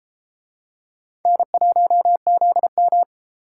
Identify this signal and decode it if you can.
Morse code, heard as a keyed tone, about 33 wpm, 700 Hz. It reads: N1ZM